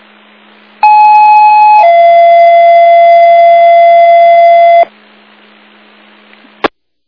Two-tone page, heard as a one-second tone, then a three-second tone, about 850 then 680 Hz. It is heard from 0.8 to 4.8 s.